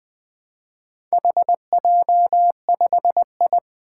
Morse code, heard as a keyed tone, about 20 wpm, 700 Hz. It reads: HJ5I
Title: Four Challenging Letter-Number Combinations 20wpm